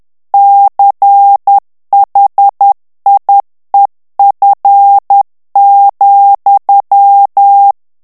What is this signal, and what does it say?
Morse code, heard as a keyed tone, about 10 wpm, 800 Hz. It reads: CHIEF,
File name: ANGLAIS texte clair 3 vitesse 10 mots minute